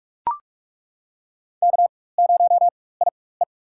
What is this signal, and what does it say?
Morse code, heard as a keyed tone, about 41 wpm, spaced as for 20 wpm, 700 Hz. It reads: K0IE
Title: USA Call Signs - 4 Characters - 4x Character Spacing - 45wpm